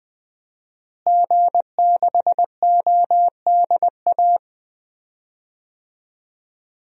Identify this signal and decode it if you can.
Morse code, heard as a keyed tone, about 20 wpm, 700 Hz. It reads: G6ODA
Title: DXCC-100 Call Signs - 5 Characters - Repeated 2x - 20wpm